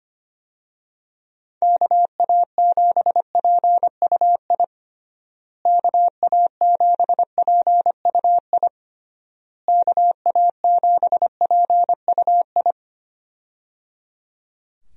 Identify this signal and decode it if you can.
Morse code, heard as a keyed tone, about 25 wpm, 700 Hz. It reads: KA7PUI KA7PUI KA7PUI
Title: USA Call Signs - 6 Characters - Repeated 3x - 25wpm